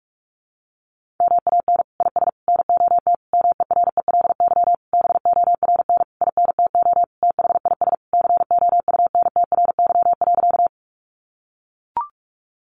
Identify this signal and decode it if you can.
Morse code, heard as a keyed tone, about 44 wpm, 700 Hz. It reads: MAN IS NOT MERELY BORN INTO THIS COUNTRY.
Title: Sentences from Top 1400 Words 45wpm